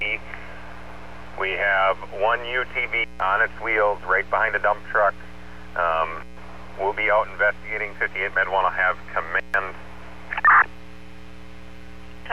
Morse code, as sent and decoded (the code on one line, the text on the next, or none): none